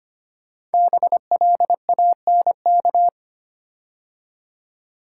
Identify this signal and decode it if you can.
Morse code, heard as a keyed tone, about 25 wpm, 700 Hz. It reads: BLANK
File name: 5 Letter Words - First 3 Letters Given - 25wpm